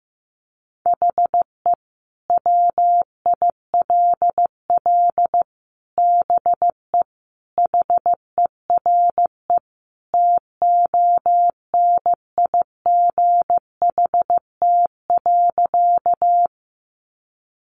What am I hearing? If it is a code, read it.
Morse code, heard as a keyed tone, about 15 wpm, 700 Hz. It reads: HE WILL BE HERE TONIGHT.
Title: Sentences from Top 600 Words 15wpm